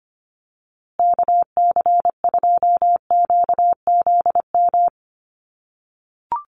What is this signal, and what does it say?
Morse code, heard as a keyed tone, about 25 wpm, 700 Hz. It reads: KC2QZM
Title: USA Call Signs Repeated 2x 25wpm